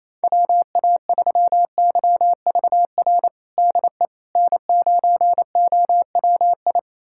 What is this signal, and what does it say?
Morse code, heard as a keyed tone, about 28 wpm, 700 Hz. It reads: WA3YVR DE N9OWI
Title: USA Callsign DE Callsign 28wpm